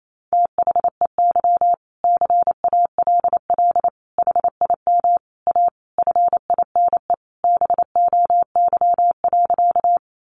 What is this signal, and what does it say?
Morse code, heard as a keyed tone, about 28 wpm, 700 Hz. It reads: THEY CALL HIM A FINE BOY.
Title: Sentences from Top 500 Words 28wpm